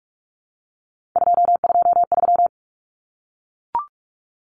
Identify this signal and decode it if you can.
Morse code, heard as a keyed tone, about 44 wpm, 700 Hz. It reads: 223
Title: Common Cut Numbers - 3-Digit 45wpm